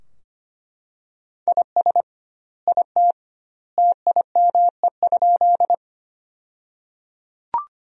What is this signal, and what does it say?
Morse code, heard as a keyed tone, about 25 wpm, 700 Hz. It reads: IS IT TIME?
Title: 2x Word-Spacing - Sentences from Top 100 Words - 25wpm